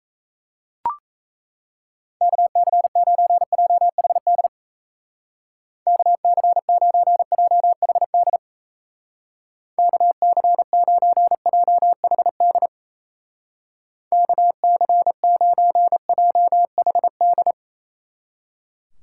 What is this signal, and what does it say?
Morse code, heard as a keyed tone, about 33 wpm, 700 Hz. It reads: KC9JHD KC9JHD KC9JHD KC9JHD
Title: Speed Racing - USA 2x3 Call Signs 28wpm